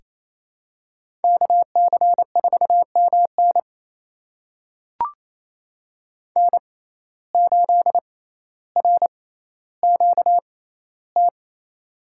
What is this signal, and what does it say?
Morse code, heard as a keyed tone, about 28 wpm, spaced as for 14 wpm, 700 Hz. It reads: KC4MN N8RQT
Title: USA Call Signs - 5 Characters - 6x Character Spacing - 28wpm